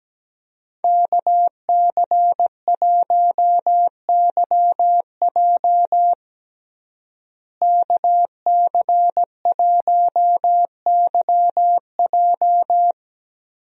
Morse code, heard as a keyed tone, about 17 wpm, 700 Hz. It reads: KC1YJ KC1YJ
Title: USA Call Signs - 5 Characters - Repeated 3x - 17wpm